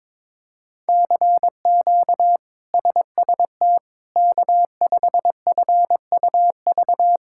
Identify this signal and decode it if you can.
Morse code, heard as a keyed tone, about 22 wpm, 700 Hz. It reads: CQ SST K5FUV
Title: K1USN SST QSO Practice - 22wpm